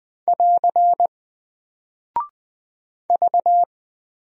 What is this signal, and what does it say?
Morse code, heard as a keyed tone, about 20 wpm, 700 Hz. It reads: <AR> V
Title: Single Character (Letter:Number:Punctuation:Prosign) Alt 20wpm